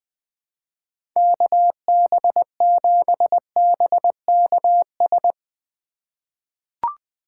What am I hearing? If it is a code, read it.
Morse code, heard as a keyed tone, about 21 wpm, 700 Hz. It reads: KB7BKS
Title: USA Call Signs - 3x Character Spacing - 20wpm